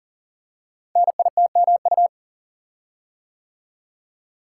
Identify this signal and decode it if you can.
Morse code, heard as a keyed tone, about 42 wpm, 690 Hz. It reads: NITMU
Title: Code Groups - 5 Letter:Numbers - 40wpm